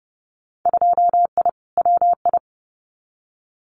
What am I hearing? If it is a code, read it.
Morse code, heard as a keyed tone, about 30 wpm, 700 Hz. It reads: 2I WI
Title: Winter Field Day Exchanges 30wpm